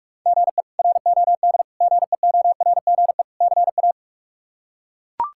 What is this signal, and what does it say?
Morse code, heard as a keyed tone, about 42 wpm, 700 Hz. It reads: GE ROD GEORGE CA
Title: K1USN SST QSO Practice - 45wpm